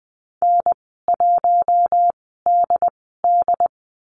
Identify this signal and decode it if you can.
Morse code, heard as a keyed tone, about 20 wpm, spaced as for 15 wpm, 700 Hz. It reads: N1DD
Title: 147 Lesson 24g - Letter y - Call Signs 20wpm